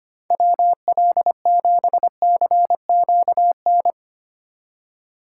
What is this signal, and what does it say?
Morse code, heard as a keyed tone, about 25 wpm, 700 Hz. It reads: WL7CQN
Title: USA Call Signs - 6 Characters - 3x Character Spacing - 25wpm